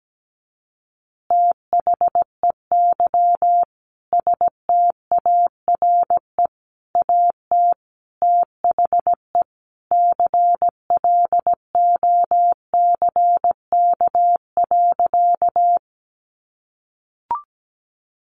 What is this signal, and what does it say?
Morse code, heard as a keyed tone, about 17 wpm, 700 Hz. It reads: THEY STARE AT THE CLOCK.
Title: Easy Phrases 17wpm